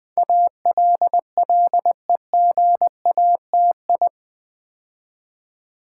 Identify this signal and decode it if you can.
Morse code, heard as a keyed tone, about 20 wpm, 700 Hz. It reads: ALLEGATI
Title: Build A Word - 4 Letters At A Time - 10-Letter Words - 20wpm